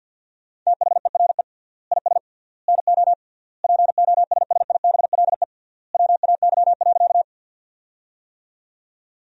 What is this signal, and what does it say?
Morse code, heard as a keyed tone, about 50 wpm, 700 Hz. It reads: THERE IS NO POSSIBLE WAY.
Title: 3x Word-Spacing - Sentences from Top 500 Words - 50wpm